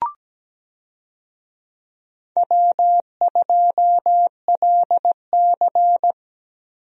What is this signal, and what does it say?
Morse code, heard as a keyed tone, about 17 wpm, 700 Hz. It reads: W2LC
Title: USA Call Signs - 4 Characters - Repeated 2x - 17wpm